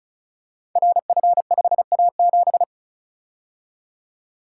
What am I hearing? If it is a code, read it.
Morse code, heard as a keyed tone, about 36 wpm, 690 Hz. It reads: RF5A7
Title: Code Groups - 5 Letter:Numbers - 35wpm